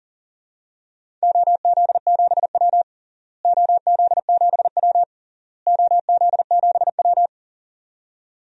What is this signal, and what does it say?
Morse code, heard as a keyed tone, about 40 wpm, 700 Hz. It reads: OZ7W OZ7W OZ7W
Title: DXCC-100 Call Signs - 4 Characters - Repeated 3x - 40wpm